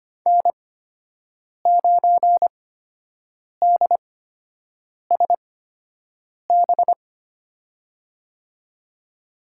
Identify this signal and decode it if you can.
Morse code, heard as a keyed tone, about 25 wpm, spaced as for 7 wpm, 700 Hz. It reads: N9DSB
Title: USA Call Signs - 5 Characters - 8x Character Spacing - 25wpm